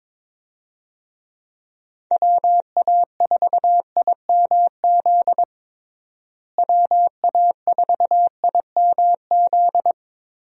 Morse code, heard as a keyed tone, about 22 wpm, 700 Hz. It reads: WA4IMZ WA4IMZ
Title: USA Call Signs - 6 Characters - Repeated 3x - 22wpm